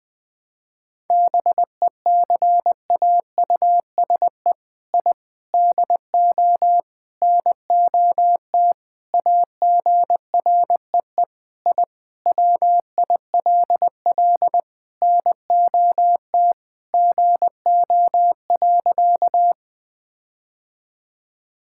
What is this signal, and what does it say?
Morse code, heard as a keyed tone, about 20 wpm, 700 Hz. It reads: BECAUSE I DO NOT AGREE I WILL NOT GO.
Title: Sentences with 4-Word Phrases - 20wpm